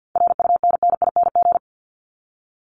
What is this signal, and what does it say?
Morse code, heard as a keyed tone, about 50 wpm, 700 Hz. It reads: RUNNING
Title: ING Suffix - Top 100 Words with Base Words 50wpm